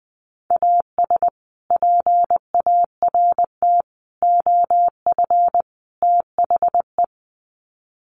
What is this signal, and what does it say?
Morse code, heard as a keyed tone, about 20 wpm, 700 Hz. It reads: AS PART OF THE
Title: Sets of 4 Words 20wpm